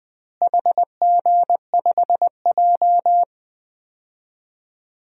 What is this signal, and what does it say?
Morse code, heard as a keyed tone, about 20 wpm, 700 Hz. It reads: HG5J
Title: DXCC-100 Call Signs - 4 Characters - Repeated 3x - 20wpm